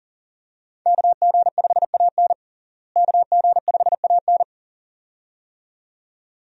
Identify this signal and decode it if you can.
Morse code, heard as a keyed tone, about 40 wpm, 700 Hz. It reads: KG5AN KG5AN
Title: USA Call Signs - 5 Characters - Repeated 2x - 40wpm